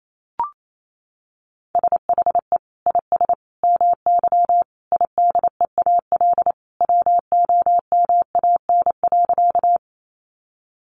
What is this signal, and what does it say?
Morse code, heard as a keyed tone, about 28 wpm, 700 Hz. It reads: SHE IS MY IDEAL WOMAN.
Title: Sentences from Top 1400 Words 28wpm